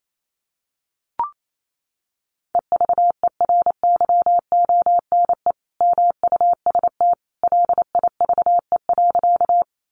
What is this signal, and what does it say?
Morse code, heard as a keyed tone, about 28 wpm, 700 Hz. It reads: EVERYONE MUST LIVE.
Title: Sentences from Top 800 Words 28wpm